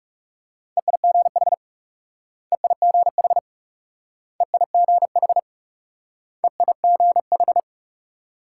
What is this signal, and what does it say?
Morse code, heard as a keyed tone, about 37 wpm, 700 Hz. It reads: EIGH EIGH EIGH EIGH
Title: Speed-Racing Multi-letter Phonemes - 30wpm